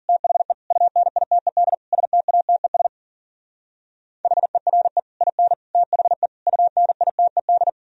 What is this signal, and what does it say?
Morse code, heard as a keyed tone, about 50 wpm, 700 Hz. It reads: THE UNITED STATES HERE IN THE UNITED
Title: Speed-Racing - Sets of 5 Words 40wpm